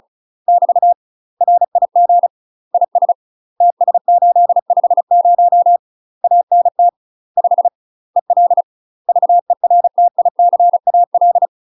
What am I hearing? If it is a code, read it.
Morse code, heard as a keyed tone, about 34 wpm, 700 Hz. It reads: <BT> RIG IS TS850 ANT 5 EL VERTICAL
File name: UK MASTER QSOs TRAINING / 35WPM / UK MASTER QSOs 0005